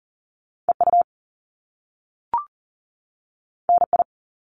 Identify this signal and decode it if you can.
Morse code, heard as a keyed tone, about 40 wpm, 700 Hz. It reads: EU NI
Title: Top Two Letter Combinations in Words 40wpm